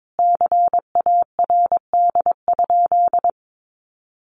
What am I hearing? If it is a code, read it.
Morse code, heard as a keyed tone, about 22 wpm, 700 Hz. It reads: CARD?
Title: Sentences from Top 1400 Words 22wpm